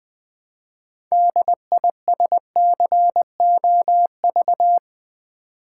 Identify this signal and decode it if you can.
Morse code, heard as a keyed tone, about 20 wpm, 700 Hz. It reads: DISCOV
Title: Build A Word - 3 Letters At A Time - 9-Letter Words - 20wpm